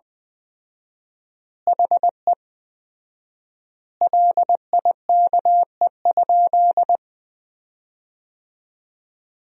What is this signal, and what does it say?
Morse code, heard as a keyed tone, about 20 wpm, 700 Hz. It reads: HE LIKE?
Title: 4x Word-Spacing - Sentences from Top 100 Words - 20wpm